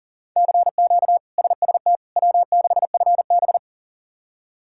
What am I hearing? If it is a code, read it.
Morse code, heard as a keyed tone, about 37 wpm, 700 Hz. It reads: CQ SST W6FB
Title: K1USN SST QSO Practice - 40wpm